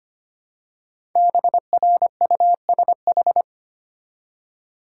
Morse code, heard as a keyed tone, about 25 wpm, 700 Hz. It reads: BRUSH
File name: 5 Letter Words - First 3 Letters Given - 25wpm